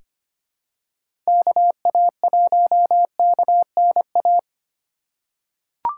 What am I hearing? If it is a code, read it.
Morse code, heard as a keyed tone, about 25 wpm, 700 Hz. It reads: KA1KNA
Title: USA Call Signs - 6 Characters - 2x Character Spacing - 25wpm